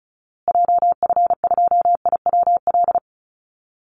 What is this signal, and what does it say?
Morse code, heard as a keyed tone, about 34 wpm, 700 Hz. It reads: JF2IWL
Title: Call Signs Repeated 2x - 35wpm